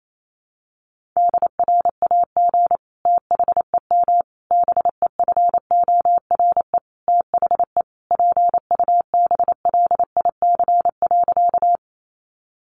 Morse code, heard as a keyed tone, about 28 wpm, 700 Hz. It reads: DRAG THEM BEFORE THE PUBLIC.
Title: Sentences from Top 1500 Words 28wpm